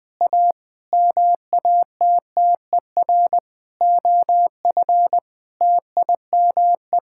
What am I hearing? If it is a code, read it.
Morse code, heard as a keyed tone, about 20 wpm, 700 Hz. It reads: A MATTER OF TIME
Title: Sets of 9 Words - 20wpm